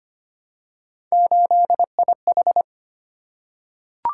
Morse code, heard as a keyed tone, about 25 wpm, 700 Hz. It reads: 8IH
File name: Three Letter-Number Combinations - 25wpm